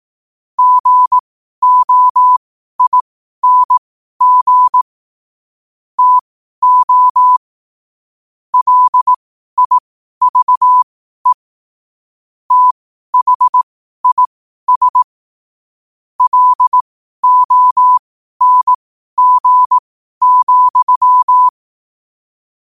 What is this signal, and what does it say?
Morse code, heard as a keyed tone, about 18 wpm, spaced as for 12 wpm, 1000 Hz. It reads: GOING TO LIVE THIS LONG,
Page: Quotes for Sun, 27 Apr 2025 in Morse Code at 12 words per minute.